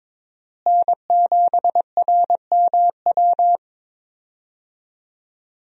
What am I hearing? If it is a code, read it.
Morse code, heard as a keyed tone, about 22 wpm, 700 Hz. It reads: N7RMW
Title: USA Call Signs - 5 Characters - Repeated 3x - 22wpm